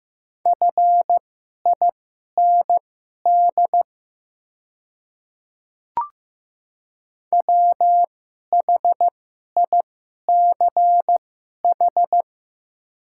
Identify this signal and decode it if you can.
Morse code, heard as a keyed tone, about 15 wpm, spaced as for 11 wpm, 700 Hz. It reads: FIND WHICH
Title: 2x Character Spacing - Top 100 Words - 15wpm